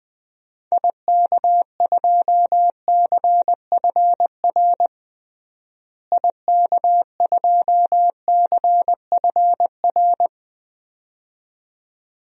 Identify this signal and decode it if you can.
Morse code, heard as a keyed tone, about 20 wpm, 700 Hz. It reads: IK2CFR IK2CFR
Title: DXCC-100 Call Signs - 6 Characters - Repeated 2x - 20wpm